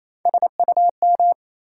Morse code, heard as a keyed tone, about 28 wpm, 700 Hz. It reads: SUM
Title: Sentences from Top 1000 Words - Review Part 6 - New Word Drill - 28wpm